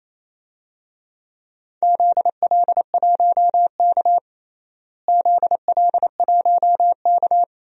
Morse code, heard as a keyed tone, about 28 wpm, 700 Hz. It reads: ZL1K ZL1K
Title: DXCC-100 Call Signs - 4 Characters - Repeated 2x - 28wpm